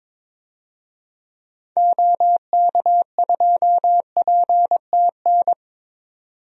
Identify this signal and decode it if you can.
Morse code, heard as a keyed tone, about 22 wpm, 700 Hz. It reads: OK2PTN
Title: DXCC-100 Call Signs - 6 Characters - 22wpm